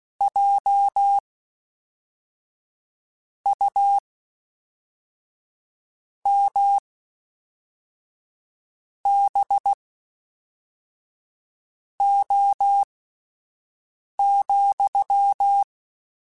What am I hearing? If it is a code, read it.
Morse code, heard as a keyed tone, about 16 wpm, spaced as for 4 wpm, 800 Hz. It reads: JUMBO,
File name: Misc / Misc-mp3s / WordPairs sp @5